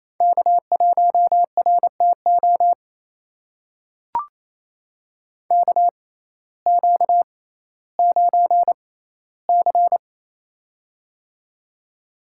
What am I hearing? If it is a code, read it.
Morse code, heard as a keyed tone, about 28 wpm, spaced as for 15 wpm, 700 Hz. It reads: K1RTO KQ9C
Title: USA Call Signs - 6x Character Spacing - 28wpm